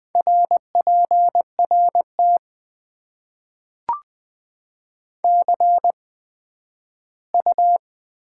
Morse code, heard as a keyed tone, about 20 wpm, spaced as for 12 wpm, 700 Hz. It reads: RPRT CU
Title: Top 90 QSO Elements - 8x Character Spacing - 20wpm